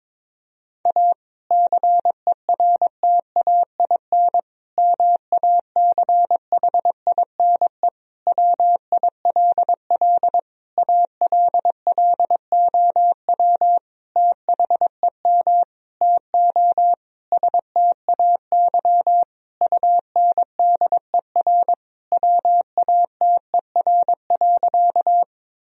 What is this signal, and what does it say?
Morse code, heard as a keyed tone, about 22 wpm, 700 Hz. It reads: A CERTAIN MACHINE WILL ALLOW THEM TO STAY UNDER WATER.
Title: Sentences from Top 1000 Words - Review Part 2 - 22wpm